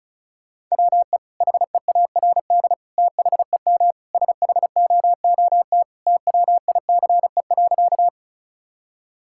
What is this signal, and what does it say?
Morse code, heard as a keyed tone, about 36 wpm, 700 Hz. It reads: WE HEARD THEM SHOOT TWICE.
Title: Sentences from Top 1600 Words 35wpm